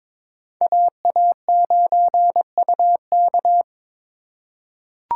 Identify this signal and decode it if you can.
Morse code, heard as a keyed tone, about 22 wpm, 700 Hz. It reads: AA9UK
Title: USA Call Signs - 5 Characters - 3x Character Spacing - 22wpm